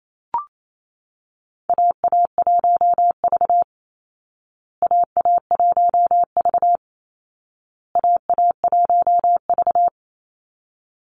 Morse code, heard as a keyed tone, about 28 wpm, 700 Hz. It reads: AA1V AA1V AA1V
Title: Call Signs Repeated 2x - 28wpm